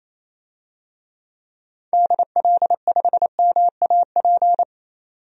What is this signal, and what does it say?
Morse code, heard as a keyed tone, about 28 wpm, 700 Hz. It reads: DL5MAP
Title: DXCC-100 Call Signs - 6 Characters - 28wpm